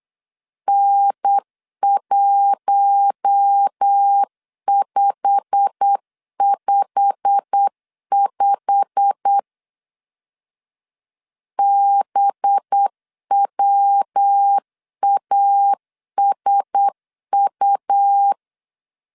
Morse code, heard as a keyed tone, about 8 wpm, 800 Hz. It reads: N1555 BWASU
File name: LEZIONI / 8 YJ45 / 8 YJ45 8wpm 3214